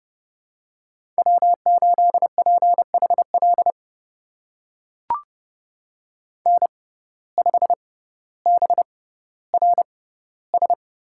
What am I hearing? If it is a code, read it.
Morse code, heard as a keyed tone, about 30 wpm, spaced as for 15 wpm, 700 Hz. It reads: W8PHL N5BRS
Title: USA Call Signs - 5 Characters - 6x Character Spacing - 30wpm